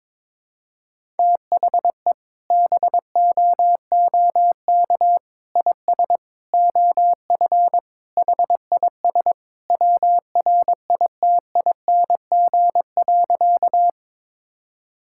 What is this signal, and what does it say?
Morse code, heard as a keyed tone, about 22 wpm, 700 Hz. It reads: THE BOOK IS OF HIS WRITING.
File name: practice / 22wpm / Sentences with 2-Word Phrases - 22wpm